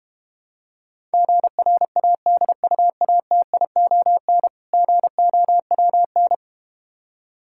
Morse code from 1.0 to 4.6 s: GRADUATION

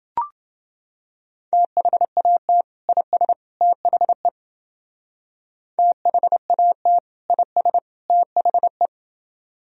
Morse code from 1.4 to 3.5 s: THAT IS

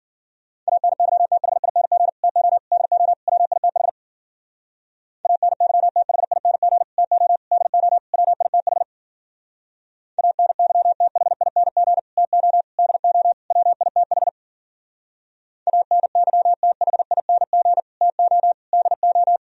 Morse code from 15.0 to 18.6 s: ANYTHING TO

